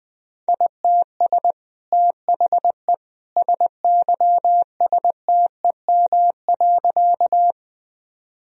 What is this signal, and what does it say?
Morse code, heard as a keyed tone, about 20 wpm, 700 Hz. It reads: ITS THE SYSTEM.
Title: Sentences from Top 1000 Words - Review Part 3 - 20wpm